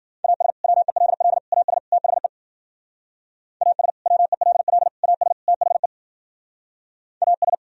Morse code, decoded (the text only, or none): AS WELL AS THE AS WELL AS THE AS